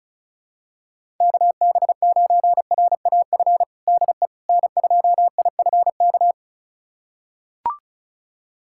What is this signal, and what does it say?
Morse code, heard as a keyed tone, about 36 wpm, 700 Hz. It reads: KB9RAF DE N2IFK
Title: USA Callsign DE Callsign 35wpm